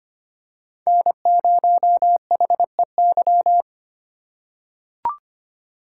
Morse code, heard as a keyed tone, about 25 wpm, 700 Hz. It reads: N0HEY
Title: USA Call Signs - 5 Characters - 2x Character Spacing - 25wpm